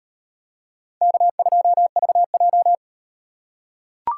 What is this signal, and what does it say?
Morse code, heard as a keyed tone, about 38 wpm, 700 Hz. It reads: K2VJ